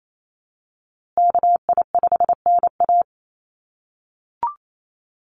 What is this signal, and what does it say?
Morse code, heard as a keyed tone, about 28 wpm, 700 Hz. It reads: KI5NA
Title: USA Call Signs Repeated 3x 28wpm